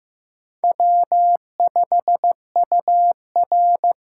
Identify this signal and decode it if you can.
Morse code, heard as a keyed tone, about 15 wpm, 700 Hz. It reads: W5UR
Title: USA Call Signs - 5 Characters - 6x Character Spacing - 15wpm